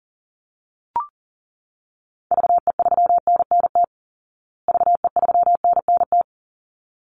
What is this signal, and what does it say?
Morse code, heard as a keyed tone, about 40 wpm, 700 Hz. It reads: VE3NNT VE3NNT
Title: Call Signs Repeated 3x - 40wpm